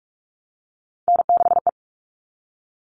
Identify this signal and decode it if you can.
Morse code, heard as a keyed tone, about 44 wpm, 700 Hz. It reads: N6E
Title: All Cut Numbers - 3-Digit 45wpm